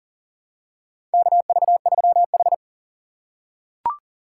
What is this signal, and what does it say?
Morse code, heard as a keyed tone, about 38 wpm, 700 Hz. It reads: KV3H
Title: USA Call Signs - 4 Characters - 2x Character Spacing - 40wpm